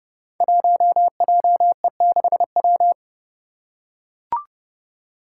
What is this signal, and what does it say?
Morse code, heard as a keyed tone, about 30 wpm, 700 Hz. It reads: 1JE6W
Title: Five Challenging Letter-Number Combinations 30wpm